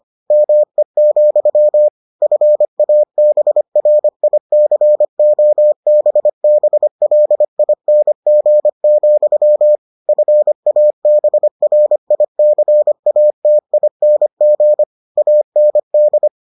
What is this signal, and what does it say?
Morse code, heard as a keyed tone, about 25 wpm, 600 Hz. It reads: ME, FABRICOBBLING, FABRICATING AND